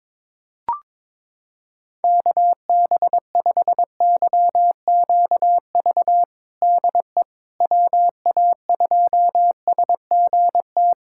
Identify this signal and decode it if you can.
Morse code, heard as a keyed tone, about 22 wpm, 700 Hz. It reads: KB5YQV DE WA2SGT
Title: USA Callsign DE Callsign 22wpm